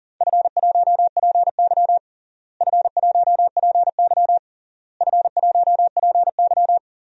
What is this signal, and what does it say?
Morse code, heard as a keyed tone, about 37 wpm, 700 Hz. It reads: F1PY F1PY F1PY
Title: DXCC-100 Call Signs - 4 Characters - Repeated 3x - 40wpm